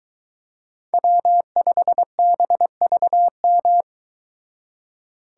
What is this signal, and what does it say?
Morse code, heard as a keyed tone, about 23 wpm, 700 Hz. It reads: W5BVM